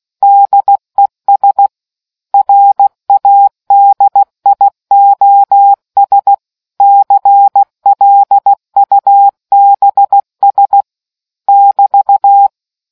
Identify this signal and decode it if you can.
Morse code, heard as a keyed tone, about 16 wpm, 800 Hz. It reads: DES RADIOS CLUBS <BT>